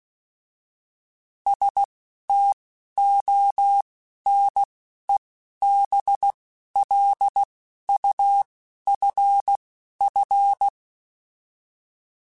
Morse code, heard as a keyed tone, about 16 wpm, spaced as for 11 wpm, 800 Hz. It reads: STONEBLUFF